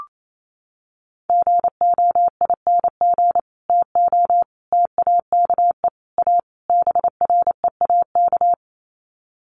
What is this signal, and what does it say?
Morse code, heard as a keyed tone, about 28 wpm, 700 Hz. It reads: GOING TO TAKE A BREAK
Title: Sets of 5 Words 28wpm